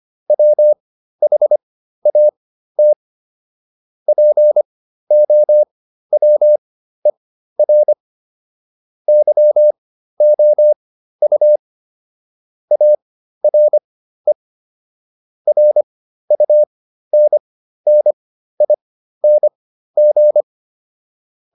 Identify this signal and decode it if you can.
Morse code, heard as a keyed tone, about 25 wpm, spaced as for 13 wpm, 600 Hz. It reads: WHAT POWER YOU ARE RUNNING
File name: QSO409 13